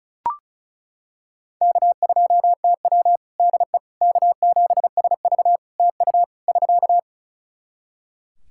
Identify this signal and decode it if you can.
Morse code, heard as a keyed tone, about 36 wpm, 700 Hz. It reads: K2TW DE K7SV TU <SK>
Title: K1USN SST QSO Practice 35wpm